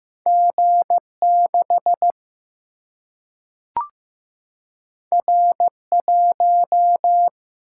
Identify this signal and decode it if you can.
Morse code, heard as a keyed tone, about 15 wpm, 700 Hz. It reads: G 6 R 1
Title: Single-Letter Single-Number 15wpm